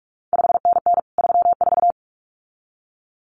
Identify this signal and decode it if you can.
Morse code, heard as a keyed tone, about 46 wpm, 700 Hz. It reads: 5NN 34
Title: CQ WW DX Contest Exchange Encore - 45wpm